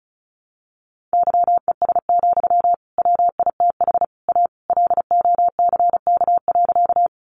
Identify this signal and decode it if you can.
Morse code, heard as a keyed tone, about 34 wpm, 700 Hz. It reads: YES, WITH A LOCK.
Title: Sentences from Top 1300 Words 35wpm